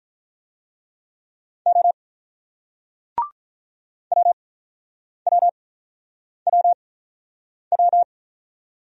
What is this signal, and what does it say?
Morse code, heard as a keyed tone, about 48 wpm, spaced as for 9 wpm, 700 Hz. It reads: O WWWW